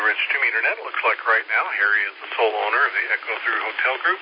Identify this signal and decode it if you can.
Emergency Alert System (SAME) tones, heard as harsh buzzing bursts, with speech. It is heard from 0.3 to 3.3 s.